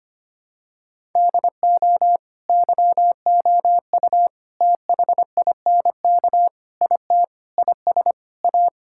Morse code, heard as a keyed tone, about 25 wpm, 700 Hz. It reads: DO YOU THINK IT IS A
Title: Sentences from Top 1000 Words - Review Part 6 - 25wpm